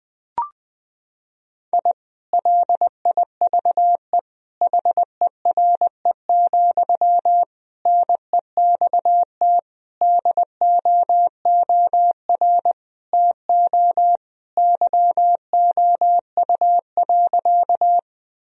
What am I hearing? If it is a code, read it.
Morse code, heard as a keyed tone, about 21 wpm, 700 Hz. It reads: I LIVE HERE, NEXT DOOR TO YOU.
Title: Sentences from Top 1000 Words - Review Part 1 - 20wpm